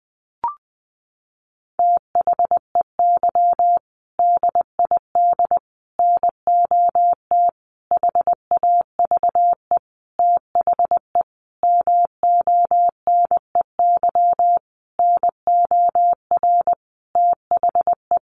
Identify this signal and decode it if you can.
Morse code, heard as a keyed tone, about 20 wpm, 700 Hz. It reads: THEY DID NOT HAVE THE MONEY NOR THE
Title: Sentences from Top 1500 Words 20wpm